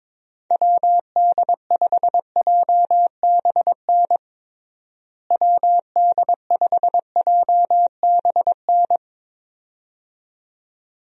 Morse code, heard as a keyed tone, about 22 wpm, 700 Hz. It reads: WD5JBN WD5JBN
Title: USA Call Signs - 6 Characters - Repeated 2x - 22wpm